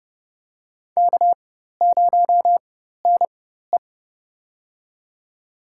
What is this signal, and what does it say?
Morse code, heard as a keyed tone, about 29 wpm, spaced as for 14 wpm, 700 Hz. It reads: K0NE
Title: USA Call Signs - 4 Characters - 4x Character Spacing - 30wpm